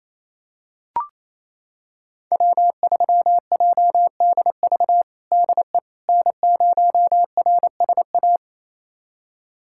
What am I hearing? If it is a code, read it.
Morse code, heard as a keyed tone, about 28 wpm, 700 Hz. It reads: W3JDV DE N0RSA
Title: USA Callsign DE Callsign - 5 Characters - 28wpm